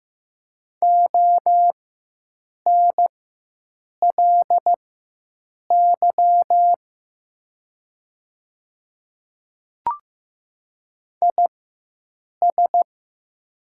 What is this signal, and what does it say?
Morse code, heard as a keyed tone, about 15 wpm, spaced as for 7 wpm, 700 Hz. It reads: ONLY IS
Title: 4x Character Spacing - Top 200 Words - 15wpm